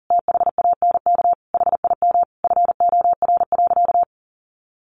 Morse code, heard as a keyed tone, about 37 wpm, 700 Hz. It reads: THANK HIM FOR.
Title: Sentences from Top 1000 Words 40wpm